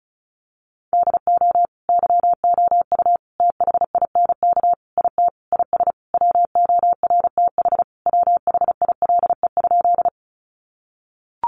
Morse code, heard as a keyed tone, about 34 wpm, 700 Hz. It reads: DO YOU THINK IT IS WORTH WHILE?
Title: Sentences from Top 800 Words 35wpm